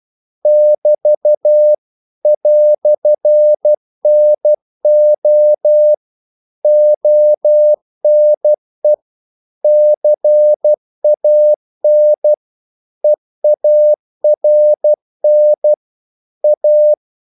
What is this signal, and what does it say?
Morse code, heard as a keyed tone, about 12 wpm, 600 Hz. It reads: <BT> "NO ONE CAN EARN A
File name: CW / Practice / EnglishQuotations0312